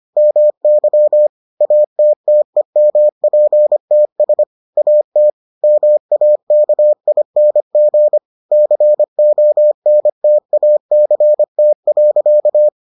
Morse code, heard as a keyed tone, about 25 wpm, 600 Hz. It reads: MY ATTEMPTS AT MAKING CONTACT.